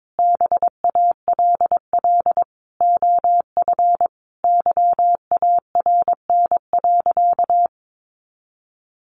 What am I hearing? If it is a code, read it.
Morse code, heard as a keyed tone, about 22 wpm, 700 Hz. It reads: BALL OF YARN.
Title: Additional Easy Phrases 22wpm